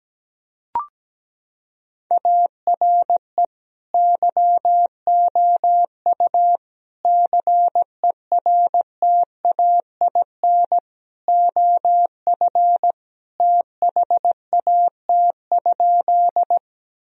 Morse code, heard as a keyed tone, about 17 wpm, 700 Hz. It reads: ARE YOU CERTAIN OF THAT?
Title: Sentences from Top 1000 Words - Review Part 2 - 17wpm